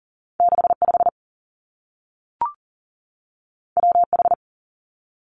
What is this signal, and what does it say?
Morse code, heard as a keyed tone, about 39 wpm, 700 Hz. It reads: 65 WH
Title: Two Challenging Letter-Number Combinations 40wpm